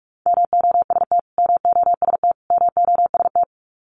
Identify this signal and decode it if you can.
Morse code, heard as a keyed tone, about 44 wpm, 700 Hz. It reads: MOST MOST MOST
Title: Top 500 Words Repeated 45wpm